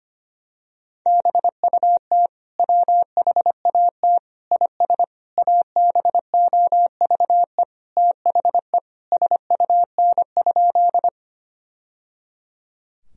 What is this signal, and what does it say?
Morse code, heard as a keyed tone, about 25 wpm, 700 Hz. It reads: BUT WHAT IS ABOVE THE SUN?
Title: Sentences from Top 1000 Words - Review Part 2 - 25wpm